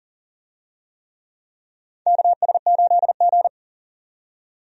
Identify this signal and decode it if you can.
Morse code, heard as a keyed tone, about 42 wpm, 690 Hz. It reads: KS8G
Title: USA Call Signs - 4 Characters - 40wpm